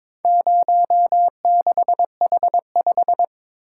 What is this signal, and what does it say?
Morse code, heard as a keyed tone, about 22 wpm, 700 Hz. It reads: 06H5
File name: practice / 22wpm / Four Letter-Number Combinations - 22wpm